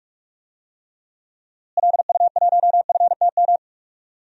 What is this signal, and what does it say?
Morse code, heard as a keyed tone, about 46 wpm, 700 Hz. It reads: LU1FTM
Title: DXCC-100 Call Signs - 6 Characters - Repeated 3x - 45wpm